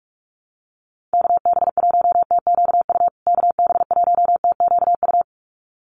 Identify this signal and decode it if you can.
Morse code, heard as a keyed tone, about 44 wpm, 700 Hz. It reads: KB1TQU KB1TQU
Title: USA Call Signs Repeated 2x 45wpm